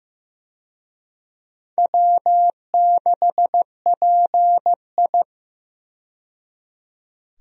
Morse code, heard as a keyed tone, about 15 wpm, 700 Hz. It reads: W6PI
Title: USA Call Signs - 4 Characters - 15wpm